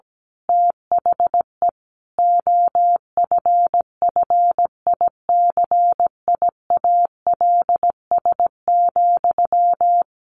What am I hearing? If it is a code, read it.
Morse code, heard as a keyed tone, about 17 wpm, 700 Hz. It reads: THE OFFICIALS,
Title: A Sound Of Thunder 17wpm